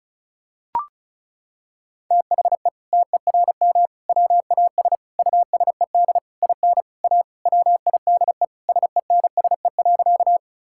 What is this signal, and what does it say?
Morse code, heard as a keyed tone, about 34 wpm, 700 Hz. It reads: THE TERM WAS USED IN A WIDE SENSE.
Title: Sentences from Top 1000 Words - Review Part 6 - 35wpm